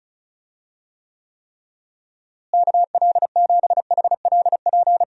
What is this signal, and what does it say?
Morse code, heard as a keyed tone, about 34 wpm, 700 Hz. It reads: KL7HLP
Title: DXCC-100 Call Signs - 6 Characters - Repeated 3x - 35wpm